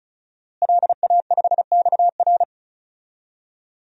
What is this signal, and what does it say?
Morse code, heard as a keyed tone, about 34 wpm, 700 Hz. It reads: LA5XR
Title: DXCC-100 Call Signs - 5 Characters - Repeated 3x - 35wpm